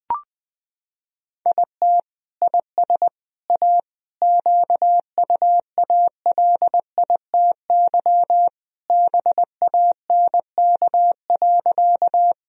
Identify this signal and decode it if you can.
Morse code, heard as a keyed tone, about 20 wpm, 700 Hz. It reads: IT IS A QUALITY BANK.
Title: Sentences from Top 1000 Words - Review Part 4 - 20wpm